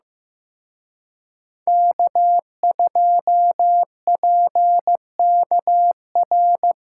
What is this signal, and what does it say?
Morse code, heard as a keyed tone, about 18 wpm, 700 Hz. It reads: K2PKR
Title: Speed Racing - USA 1x3 Call Signs 15wpm